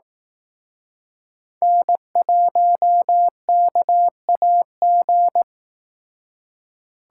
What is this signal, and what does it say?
Morse code, heard as a keyed tone, about 18 wpm, 700 Hz. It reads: N1KAG
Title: Speed Racing - USA Call Signs 15wpm